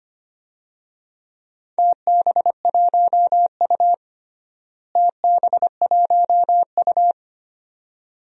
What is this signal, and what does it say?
Morse code, heard as a keyed tone, about 25 wpm, 700 Hz. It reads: TB1U TB1U
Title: DXCC-100 Call Signs - 4 Characters - Repeated 2x - 25wpm